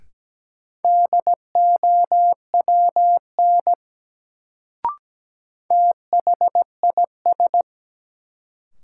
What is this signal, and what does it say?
Morse code, heard as a keyed tone, about 17 wpm, 700 Hz. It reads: DOWN THIS
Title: Sets of 2 Words - New Word Drill - 17wpm